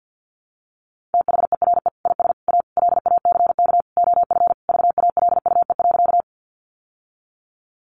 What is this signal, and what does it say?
Morse code, heard as a keyed tone, about 49 wpm, 700 Hz. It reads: THERE IS A LACK OF VALUE.
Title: Sentences from Top 1100 Words 50wpm